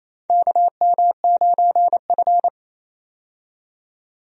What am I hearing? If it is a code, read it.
Morse code, heard as a keyed tone, about 28 wpm, 700 Hz. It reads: KM9F
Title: USA Call Signs - 4 Characters - 28wpm